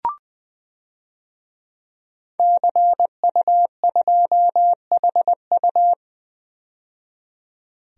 Morse code, heard as a keyed tone, about 20 wpm, 700 Hz. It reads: CU2HU
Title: DXCC-100 Call Signs - 5 Characters - 20wpm